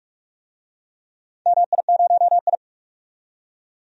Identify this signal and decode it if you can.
Morse code, heard as a keyed tone, about 41 wpm, 700 Hz. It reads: MI0I
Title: DXCC-100 Call Signs - 4 Characters - 45wpm